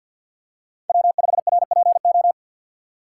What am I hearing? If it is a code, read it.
Morse code, heard as a keyed tone, about 53 wpm, 700 Hz. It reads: W5RPO
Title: USA Call Signs - 5 Characters - 6x Character Spacing - 50wpm